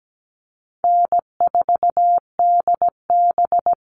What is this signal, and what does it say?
Morse code, heard as a keyed tone, about 17 wpm, 700 Hz. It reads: N4DB
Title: All Cut Numbers - 4-Digit 17wpm